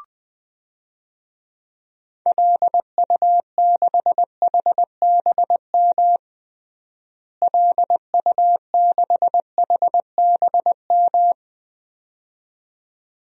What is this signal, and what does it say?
Morse code, heard as a keyed tone, about 20 wpm, 700 Hz. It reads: LU6HBM LU6HBM
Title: DXCC-100 Call Signs - 6 Characters - Repeated 2x - 20wpm